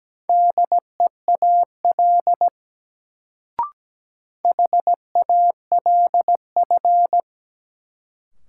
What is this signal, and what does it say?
Morse code, heard as a keyed tone, about 17 wpm, 700 Hz. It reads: DEAL HALF
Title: Sets of 3 Words - New Word Drill - 17wpm